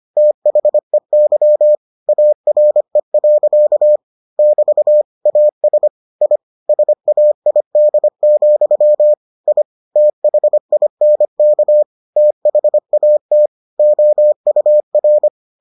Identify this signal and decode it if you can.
Morse code, heard as a keyed tone, about 25 wpm, 600 Hz. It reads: THEY ARE. <BT> AS I SAID, I THINK THAT OUR